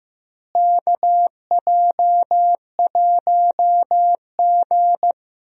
Morse code, heard as a keyed tone, about 15 wpm, 700 Hz. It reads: KJ1G
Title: USA Call Signs - 4 Characters - Repeated 3x - 15wpm